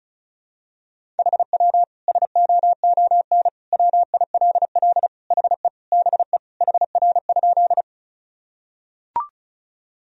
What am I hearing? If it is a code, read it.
Morse code, heard as a keyed tone, about 36 wpm, 700 Hz. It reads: HW SOON WILL HE BE HR?
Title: Sentences from Top 400 Words and Abbreviations 35wpm